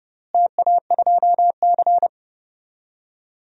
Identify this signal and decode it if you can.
Morse code, heard as a keyed tone, about 30 wpm, 700 Hz. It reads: TA2C
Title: DXCC-100 Call Signs - 4 Characters - Repeated 3x - 30wpm